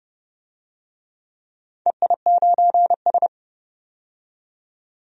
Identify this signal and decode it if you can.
Morse code, heard as a keyed tone, about 29 wpm, 700 Hz. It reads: EI9S
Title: DXCC-100 Call Signs - 4 Characters - 30wpm